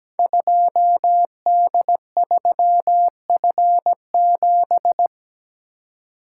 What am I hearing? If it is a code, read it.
Morse code, heard as a keyed tone, about 17 wpm, 700 Hz. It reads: 2D3F7
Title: Code Groups - 5 Letter:Numbers - 17wpm